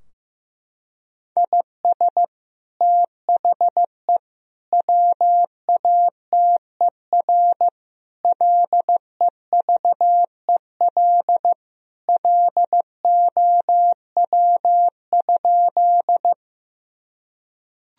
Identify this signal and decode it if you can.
Morse code, heard as a keyed tone, about 15 wpm, 700 Hz. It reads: IS THE WATER LEVEL LOW?
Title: Sentences from Top 1000 Words - Review Part 4 - 15wpm